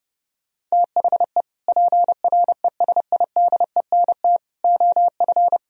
Morse code, heard as a keyed tone, about 29 wpm, 700 Hz. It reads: THE PRESIDENT OF